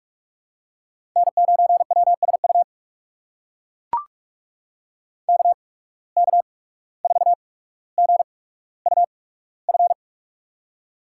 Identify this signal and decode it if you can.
Morse code, heard as a keyed tone, about 44 wpm, spaced as for 18 wpm, 700 Hz. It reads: N9WSU KK4GUF
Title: USA Call Signs - 8x Character Spacing - 45wpm